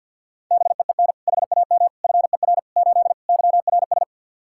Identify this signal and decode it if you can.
Morse code, heard as a keyed tone, about 47 wpm, 700 Hz. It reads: BEEN HAM FER 8 YRS
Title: Practice QSO - Exchange 3 - More Info - 50wpm